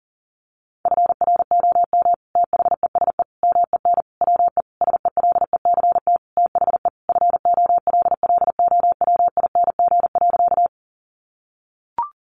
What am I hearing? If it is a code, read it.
Morse code, heard as a keyed tone, about 40 wpm, 700 Hz. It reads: FROM THESE MEN WE SELECT THE FOLLOWING.
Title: Sentences from Top 1300 Words 40wpm